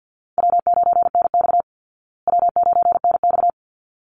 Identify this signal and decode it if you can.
Morse code, heard as a keyed tone, about 48 wpm, 700 Hz. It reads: W9NX W9NX
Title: Easy Call Signs Repeated 2x - 50wpm